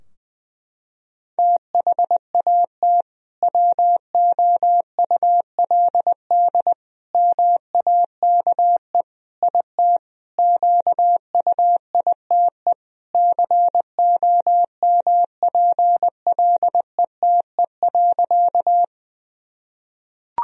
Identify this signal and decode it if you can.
Morse code, heard as a keyed tone, about 20 wpm, 700 Hz. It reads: THAT WOULD MAKE IT QUITE COMPLETE.
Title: Sentences from Top 1000 Words - Review Part 2 - 20wpm